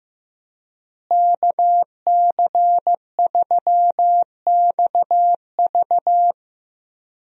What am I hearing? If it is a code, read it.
Morse code, heard as a keyed tone, about 15 wpm, 700 Hz. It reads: KC3XV
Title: USA Call Signs - 5 Characters - Repeated 2x - 15wpm